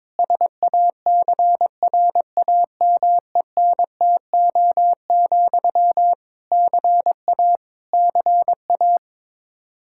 Morse code, heard as a keyed tone, about 22 wpm, 700 Hz. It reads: SACRAMENTO, CA CA
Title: K1USN SST QSO Practice 22wpm